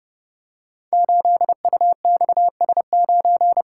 Morse code, heard as a keyed tone, about 30 wpm, 700 Hz. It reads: 8UXS9
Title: Code Groups - 5 Letter:Numbers - 30wpm